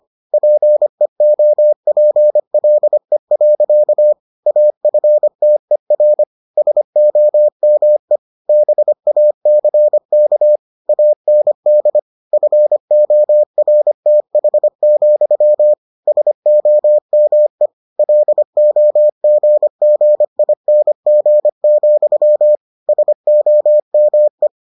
Morse code, heard as a keyed tone, about 25 wpm, 600 Hz. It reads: PEOPLE. AFTER SOME BACK AND FORTH, SOME LOGGING, SOME